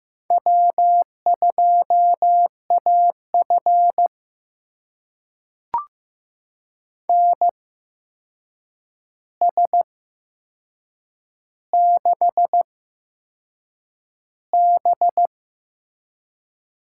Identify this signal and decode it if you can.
Morse code, heard as a keyed tone, about 15 wpm, spaced as for 6 wpm, 700 Hz. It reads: W2AF NS6B
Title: USA Call Signs - 4 Characters - 8x Character Spacing - 15wpm